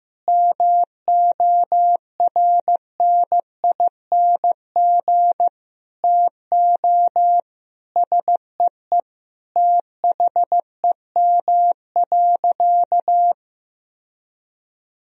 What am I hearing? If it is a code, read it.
Morse code, heard as a keyed tone, about 15 wpm, 700 Hz. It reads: MORNING TO SEE THEM.
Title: Sentences from Top 1000 Words - Review Part 1 - 15wpm